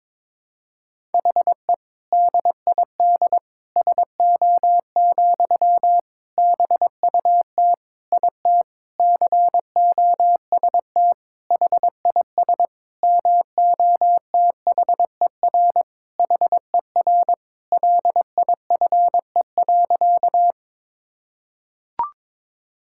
Morse code, heard as a keyed tone, about 22 wpm, 700 Hz. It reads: HE DID SO, BUT IT COST HIS MOTHER HER LIFE.
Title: Sentences from Top 1000 Words - Review Part 5 - 22wpm